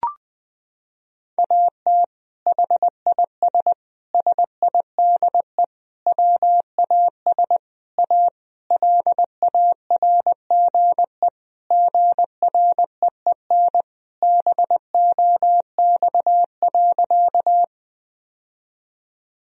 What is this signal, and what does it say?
Morse code, heard as a keyed tone, about 20 wpm, 700 Hz. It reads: AT HIS SIDE WAS A LARGE GREEN BOX.
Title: Sentences from Top 1000 Words - Review Part 2 - 20wpm